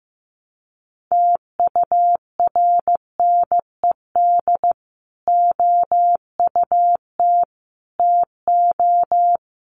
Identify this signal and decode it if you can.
Morse code, heard as a keyed tone, about 15 wpm, 700 Hz. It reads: TURNED OUT TO
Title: Sets of 4 Words 15wpm